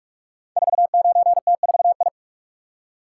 Morse code, heard as a keyed tone, about 44 wpm, 700 Hz. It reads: 49T4I
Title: Code Groups - 5 Letter:Numbers - 45wpm